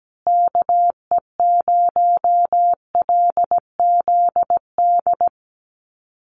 Morse code, heard as a keyed tone, about 17 wpm, 700 Hz. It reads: KE0LZD
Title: USA Call Signs Repeated 2x 17wpm